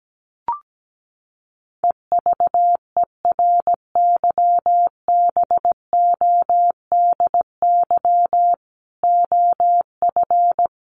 Morse code, heard as a keyed tone, about 17 wpm, 700 Hz. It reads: EVERYBODY OF
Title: Sentences from Top 1400 Words 17wpm